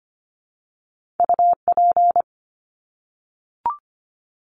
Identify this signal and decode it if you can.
Morse code, heard as a keyed tone, about 25 wpm, 700 Hz. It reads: UP
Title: Top Two Letter Combinations in Words 25wpm